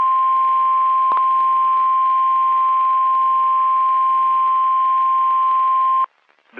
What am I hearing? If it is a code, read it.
Emergency Alert System (SAME) tones (the 1050 Hz attention tone). It is heard from 1.2 to 6.1 s.
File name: eas messages / EAS WXM31 Elmira